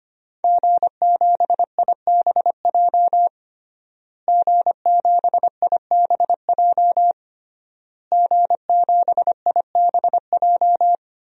Morse code, heard as a keyed tone, about 25 wpm, 700 Hz. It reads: G7IBJ G7IBJ G7IBJ
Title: DXCC-100 Call Signs - 5 Characters - Repeated 3x - 25wpm